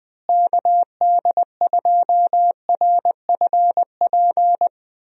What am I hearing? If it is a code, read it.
Morse code, heard as a keyed tone, about 20 wpm, 700 Hz. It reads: KD2RFP
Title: USA Call Signs - 6 Characters - Repeated 3x - 20wpm